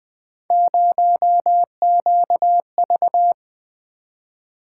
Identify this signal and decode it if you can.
Morse code, heard as a keyed tone, about 20 wpm, 700 Hz. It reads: 0QV
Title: Three Letter-Number Combinations - 20wpm